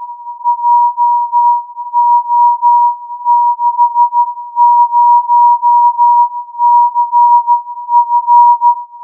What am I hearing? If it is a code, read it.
Morse code, heard as a keyed tone, about 13 wpm, 960 Hz. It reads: JO60CF